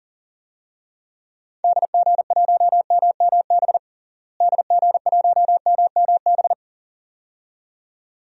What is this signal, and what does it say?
Morse code, heard as a keyed tone, about 37 wpm, 700 Hz. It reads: DG1MMB DG1MMB
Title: DXCC-100 Call Signs - 6 Characters - Repeated 2x - 40wpm